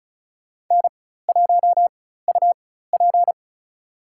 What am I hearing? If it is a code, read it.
Morse code, heard as a keyed tone, about 36 wpm, spaced as for 16 wpm, 700 Hz. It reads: N1UP
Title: USA Call Signs - 4 Characters - 4x Character Spacing - 35wpm